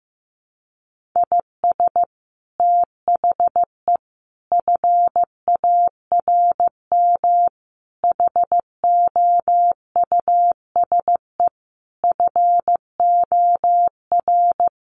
Morse code, heard as a keyed tone, about 15 wpm, 700 Hz. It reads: IS THE FARM HOUSE FOR
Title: Sentences from Top 1400 Words 15wpm